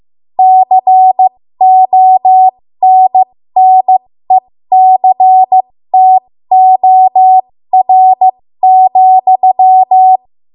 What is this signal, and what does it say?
Morse code, heard as a keyed tone, about 15 wpm, spaced as for 13 wpm, 750 Hz. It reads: CONNECTOR,
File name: Morse / 13WPM